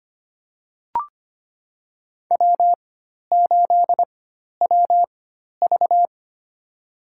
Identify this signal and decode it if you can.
Morse code, heard as a keyed tone, about 25 wpm, spaced as for 12 wpm, 700 Hz. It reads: W8WV
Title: USA Call Signs - 4 Characters - 4x Character Spacing - 25wpm